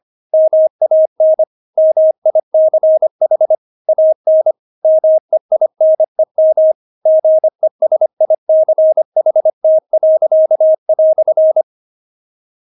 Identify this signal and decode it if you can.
Morse code, heard as a keyed tone, about 25 wpm, 630 Hz. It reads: MAN MICH AN MEINEM GESICHT."
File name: Witz6 25wpm